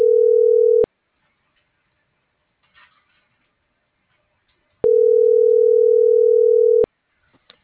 Ringback tone (North America), 440 + 480 Hz.